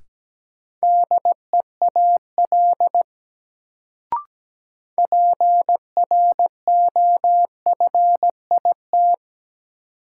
Morse code, heard as a keyed tone, about 17 wpm, 700 Hz. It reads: DEAL PROFIT
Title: Sentences from Top 800 Words - New Word Drill - 17wpm